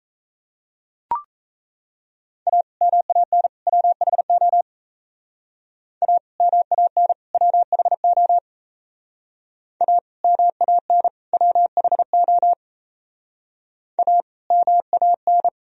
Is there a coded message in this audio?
Yes: Morse code from 2.5 to 15.5 s, heard as a keyed tone, about 37 wpm, 700 Hz. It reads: A MAN WHO A MAN WHO A MAN WHO A MAN